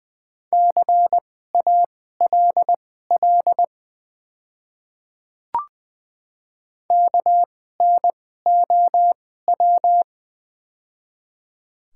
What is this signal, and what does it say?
Morse code, heard as a keyed tone, about 20 wpm, spaced as for 14 wpm, 700 Hz. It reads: CALL KNOW
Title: 2x Character Spacing - Top 100 Words - 20wpm